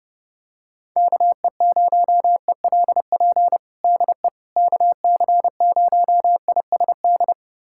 Morse code, heard as a keyed tone, about 30 wpm, 700 Hz. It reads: KE0ELP DE KC0ISD
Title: USA Callsign DE Callsign - 6 Characters - 30wpm